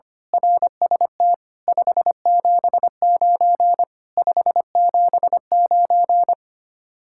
Morse code, heard as a keyed tone, about 24 wpm, 700 Hz. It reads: RST 579 579
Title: Speed-Racing Common QSO Phrases - 25wpm